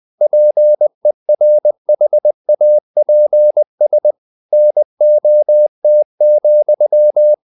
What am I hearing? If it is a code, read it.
Morse code, heard as a keyed tone, about 20 wpm, 600 Hz. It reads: PERHAPS NOT,
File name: the old man and the sea part 2 20wpm